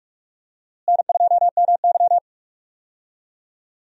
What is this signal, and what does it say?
Morse code, heard as a keyed tone, about 46 wpm, 700 Hz. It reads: N2MY
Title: USA Call Signs - 4 Characters - Repeated 2x - 45wpm